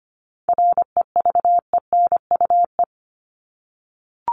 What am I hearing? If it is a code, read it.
Morse code, heard as a keyed tone, about 25 wpm, 700 Hz. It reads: REVENUE
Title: Top 1501-1750 Words 25wpm